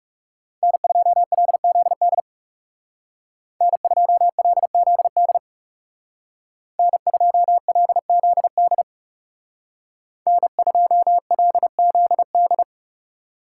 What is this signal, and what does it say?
Morse code, heard as a keyed tone, about 37 wpm, 700 Hz. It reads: N2LZD N2LZD N2LZD N2LZD
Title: Speed Racing - USA 1x3 Call Signs 30wpm